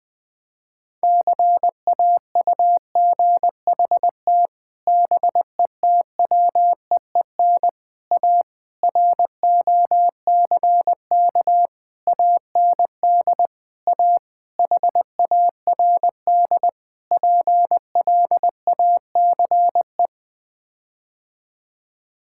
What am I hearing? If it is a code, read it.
Morse code, heard as a keyed tone, about 20 wpm, 700 Hz. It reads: CAUGHT BETWEEN A ROCK AND A HARD PLACE
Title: Sets of 8 Words - 20wpm